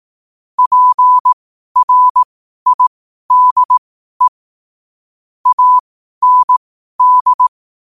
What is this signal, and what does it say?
Morse code, heard as a keyed tone, about 18 wpm, spaced as for 12 wpm, 1000 Hz. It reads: PRIDE AND
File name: qotdfiles / 12wpm